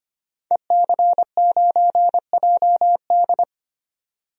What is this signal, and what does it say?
Morse code, heard as a keyed tone, about 25 wpm, 700 Hz. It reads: EC9JD
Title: DXCC-100 Call Signs - 5 Characters - 25wpm